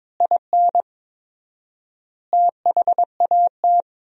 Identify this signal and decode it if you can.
Morse code, heard as a keyed tone, about 22 wpm, 700 Hz. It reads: IN THAT
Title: 4x Word-Spacing - Sentences from Top 500 Words - 22wpm